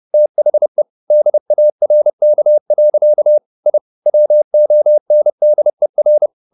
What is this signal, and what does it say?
Morse code, heard as a keyed tone, about 30 wpm, 600 Hz. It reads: THE DARK. I WONDER
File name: the old man and the sea / the old man and the sea 1 30wpm